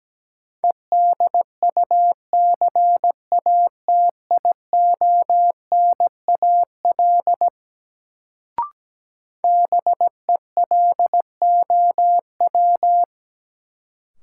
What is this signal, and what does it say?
Morse code, heard as a keyed tone, about 17 wpm, 700 Hz. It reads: EDUCATIONAL BELOW
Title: Sentences from Top 1200 Words - New Word Drill Alt - 17wpm